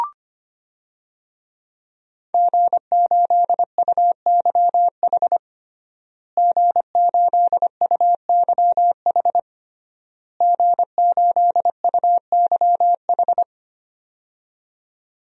Morse code, heard as a keyed tone, about 25 wpm, 700 Hz. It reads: G8UYH G8UYH G8UYH
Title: DXCC-100 Call Signs - 5 Characters - Repeated 3x - 25wpm